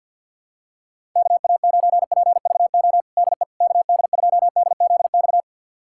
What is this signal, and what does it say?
Morse code, heard as a keyed tone, about 51 wpm, 700 Hz. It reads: KA9PVO DE KD2DZX
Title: USA Callsign DE Callsign - 6 Characters - 50wpm